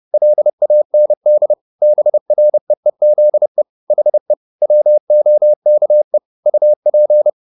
Morse code, heard as a keyed tone, about 30 wpm, 600 Hz. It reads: LAND BREEZE HE WOKE UP
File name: the old man and the sea / the old man and the sea 1 30wpm